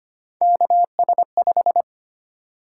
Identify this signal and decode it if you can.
Morse code, heard as a keyed tone, about 25 wpm, 700 Hz. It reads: KS5
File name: Three Letter-Number Combinations - 25wpm